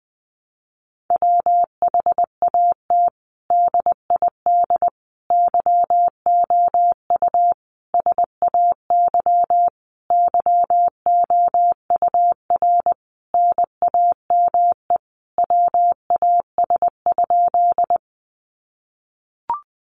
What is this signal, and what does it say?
Morse code, heard as a keyed tone, about 20 wpm, 700 Hz. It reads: WHAT DID YOU SAY YOUR NAME WAS?
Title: Sentences from Top 200 Words 20wpm